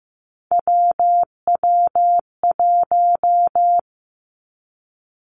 Morse code, heard as a keyed tone, about 15 wpm, 700 Hz. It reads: WW1
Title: Three Challenging Letter-Number Combinations 15wpm